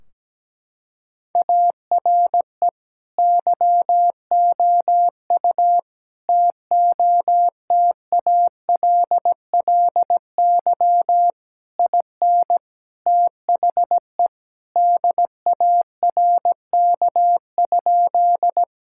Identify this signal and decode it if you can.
Morse code, heard as a keyed tone, about 17 wpm, 700 Hz. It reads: ARE YOU TOTALLY IN THE DARK?
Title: Sentences from Top 1600 Words 17wpm